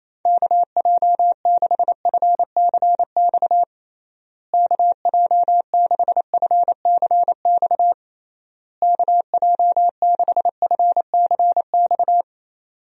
Morse code, heard as a keyed tone, about 28 wpm, 700 Hz. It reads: KJ6FCX KJ6FCX KJ6FCX
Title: USA Call Signs - 6 Characters - Repeated 3x - 28wpm